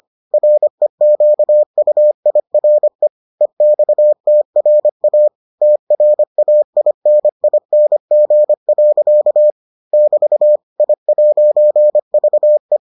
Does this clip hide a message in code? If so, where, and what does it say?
Yes: Morse code from 0.3 to 12.8 s, heard as a keyed tone, about 25 wpm, 600 Hz. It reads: REQUIRE EXTRA TRAINING. <BT> I'VE